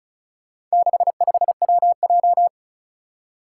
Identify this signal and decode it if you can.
Morse code, heard as a keyed tone, about 36 wpm, 700 Hz. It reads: 65WJ
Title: Four Challenging Letter-Number Combinations 35wpm